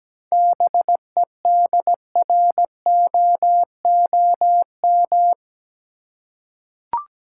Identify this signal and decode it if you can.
Morse code, heard as a keyed tone, about 17 wpm, 700 Hz. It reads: BEDROOM
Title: Seven Letter Words - 17wpm